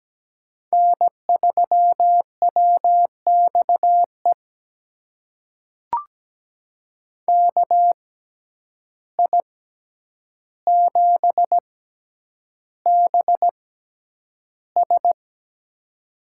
Morse code, heard as a keyed tone, about 17 wpm, spaced as for 9 wpm, 700 Hz. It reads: N3WXE KI7BS
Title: USA Call Signs - 5 Characters - 6x Character Spacing - 17wpm